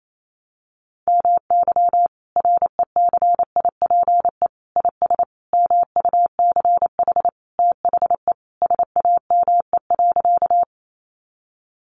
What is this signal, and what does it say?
Morse code, heard as a keyed tone, about 28 wpm, 700 Hz. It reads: MY RECIPE IS MUCH THE SAME.
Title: Sentences from Top 1200 Words 28wpm